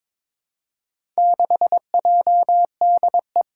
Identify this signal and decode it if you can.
Morse code, heard as a keyed tone, about 22 wpm, 700 Hz. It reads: 6JDE
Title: Four Challenging Letter-Number Combinations 22wpm